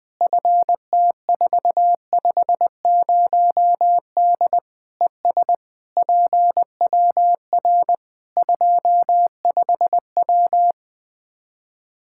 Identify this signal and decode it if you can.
Morse code, heard as a keyed tone, about 20 wpm, 700 Hz. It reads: FT450D ES PWR 25W
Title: Practice QSO - Exchange 2 - Info - 20wpm